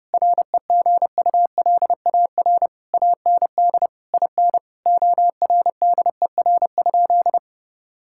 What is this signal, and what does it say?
Morse code, heard as a keyed tone, about 30 wpm, 700 Hz. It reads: REGULAR AND IN ORDER?
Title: Sentences from Top 1000 Words - Review Part 6 - 30wpm